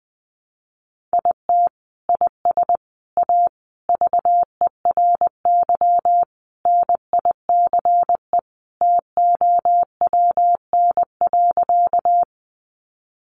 Morse code, heard as a keyed tone, about 20 wpm, 700 Hz. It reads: IT IS A VERY NICE TOWN.
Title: Sentences from Top 1100 Words 20wpm